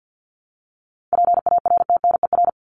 Morse code, heard as a keyed tone, about 50 wpm, 700 Hz. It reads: PARTNER
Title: Top 1501-1750 Words 50wpm